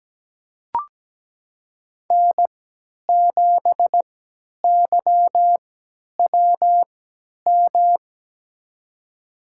Morse code, heard as a keyed tone, about 17 wpm, spaced as for 10 wpm, 700 Hz. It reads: N7YWM
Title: USA Call Signs - 5 Characters - 3x Character Spacing - 17wpm